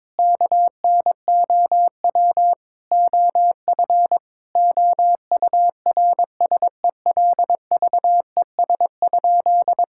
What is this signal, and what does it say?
Morse code, heard as a keyed tone, about 22 wpm, 700 Hz. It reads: KNOW OF OURSELVES?
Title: Sentences from Top 1600 Words 22wpm